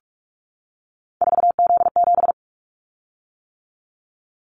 Morse code, heard as a keyed tone, about 46 wpm, 700 Hz. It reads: 4Z7
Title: Callsign Prefixes for Top 50 DXCC - 45wpm